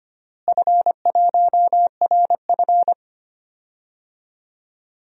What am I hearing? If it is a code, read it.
Morse code, heard as a keyed tone, about 25 wpm, 700 Hz. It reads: F1RF
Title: DXCC-100 Call Signs - 4 Characters - Repeated 2x - 25wpm